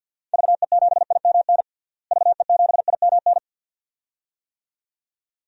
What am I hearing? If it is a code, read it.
Morse code, heard as a keyed tone, about 51 wpm, 700 Hz. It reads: VE7IMN VE7IMN
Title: DXCC-100 Call Signs - 6 Characters - Repeated 2x - 50wpm